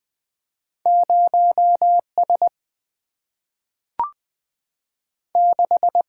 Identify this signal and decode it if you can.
Morse code, heard as a keyed tone, about 20 wpm, 700 Hz. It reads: 0 S 6
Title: Single-Letter Single-Number 20wpm